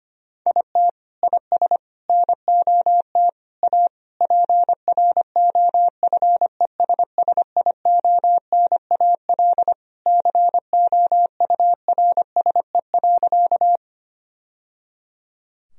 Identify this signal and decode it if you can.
Morse code, heard as a keyed tone, about 25 wpm, 700 Hz. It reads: IT IS NOT A PROFESSIONAL COURSE.
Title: Sentences from Top 1000 Words - Review Part 4 - 25wpm